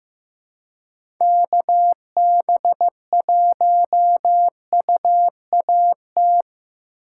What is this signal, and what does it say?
Morse code, heard as a keyed tone, about 18 wpm, 700 Hz. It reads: KB1UAT
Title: Speed Racing - USA 2x3 Call Signs 15wpm